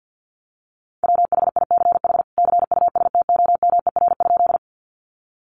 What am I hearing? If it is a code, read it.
Morse code, heard as a keyed tone, about 50 wpm, 700 Hz. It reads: WHICH CUSTOMER?
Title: Sentences from Top 1100 Words 50wpm